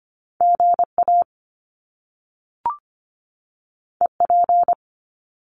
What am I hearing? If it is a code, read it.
Morse code, heard as a keyed tone, about 25 wpm, 700 Hz. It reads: GA EP
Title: Top Two Letter Combinations in Words 25wpm